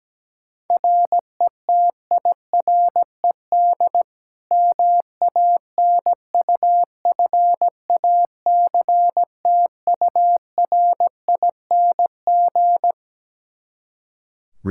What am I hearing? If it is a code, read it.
Morse code, heard as a keyed tone, about 17 wpm, 700 Hz. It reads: RETIRED MANUFACTURING
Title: K1USN SST QSO Practice 17wpm